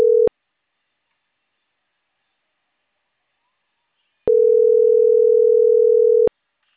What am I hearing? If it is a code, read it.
Ringback tone (North America), 440 + 480 Hz.